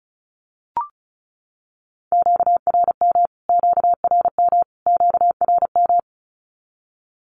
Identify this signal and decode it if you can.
Morse code, heard as a keyed tone, about 34 wpm, 700 Hz. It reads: QRM QRM QRM
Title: Q-Codes Repeated 3x 35wpm